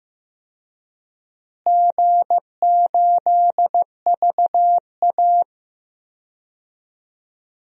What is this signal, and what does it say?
Morse code, heard as a keyed tone, about 15 wpm, 700 Hz. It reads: G8VA
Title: DXCC-100 Call Signs - 4 Characters - Repeated 3x - 15wpm